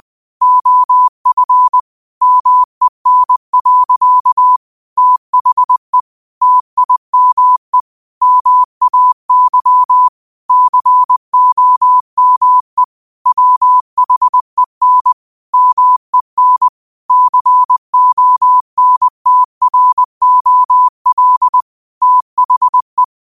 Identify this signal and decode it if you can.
Morse code, heard as a keyed tone, about 20 wpm, 1000 Hz. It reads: OF MEN. THE TIME MAY COME WHEN MEN CONTROL THE
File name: qotdfiles / 20wpm